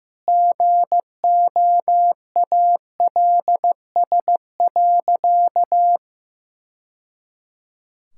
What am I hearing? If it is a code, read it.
Morse code, heard as a keyed tone, about 15 wpm, 700 Hz. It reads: GOALS.
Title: Sentences from Top 1000 Words - Review Part 6 - 15wpm